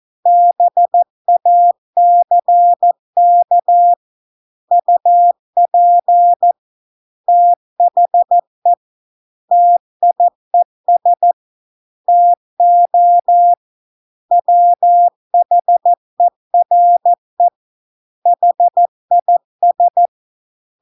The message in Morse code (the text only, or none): BACK UP THE TIES TO WHERE HIS